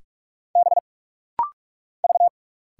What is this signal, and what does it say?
Morse code, heard as a keyed tone, about 45 wpm, 700 Hz. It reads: B V
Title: Single Letter-Number - Rapid-Fire 45wpm